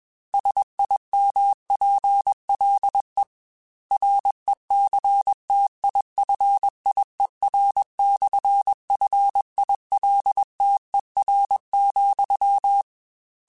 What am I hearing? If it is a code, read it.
Morse code, heard as a keyed tone, about 21 wpm, 800 Hz. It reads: SIMPLE RECTIFIER/FILTER,
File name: Technical Subjects / Technical Subjects-mp3s / Sw Pwr Sup @20